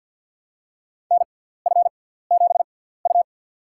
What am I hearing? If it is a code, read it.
Morse code, heard as a keyed tone, about 50 wpm, spaced as for 17 wpm, 700 Hz. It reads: NF7U
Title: USA Call Signs - 6x Character Spacing - 50wpm